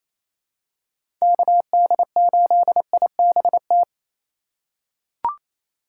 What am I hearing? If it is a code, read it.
Morse code, heard as a keyed tone, about 28 wpm, 690 Hz. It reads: KD8IBT
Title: USA Call Signs - 6 Characters - 4x Character Spacing - 28wpm